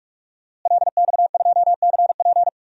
Morse code, heard as a keyed tone, about 44 wpm, 700 Hz. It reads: LX2CP
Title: DXCC-100 Call Signs - 5 Characters - Repeated 3x - 45wpm